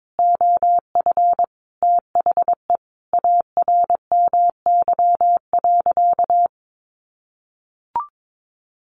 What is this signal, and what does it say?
Morse code, heard as a keyed tone, about 22 wpm, 700 Hz. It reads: OF THE ARMY.
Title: Sentences from Top 1400 Words 22wpm